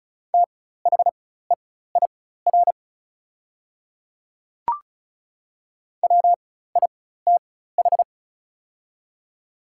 Morse code, heard as a keyed tone, about 36 wpm, spaced as for 16 wpm, 700 Hz. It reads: THEIR WITH
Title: 4x Character Spacing - Top 100 Words - 35wpm